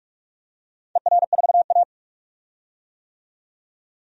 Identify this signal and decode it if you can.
Morse code, heard as a keyed tone, about 41 wpm, 700 Hz. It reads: ER4A
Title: DXCC-100 Call Signs - 4 Characters - 45wpm